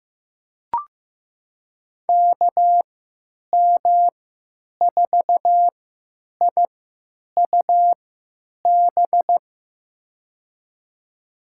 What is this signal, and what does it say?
Morse code, heard as a keyed tone, about 16 wpm, spaced as for 9 wpm, 700 Hz. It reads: KM4IUB
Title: USA Call Signs - 6 Characters - 3x Character Spacing - 15wpm